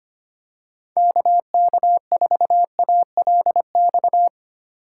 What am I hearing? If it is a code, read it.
Morse code, heard as a keyed tone, about 25 wpm, 700 Hz. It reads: KK4ALX
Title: USA Call Signs - 6 Characters - 4x Character Spacing - 25wpm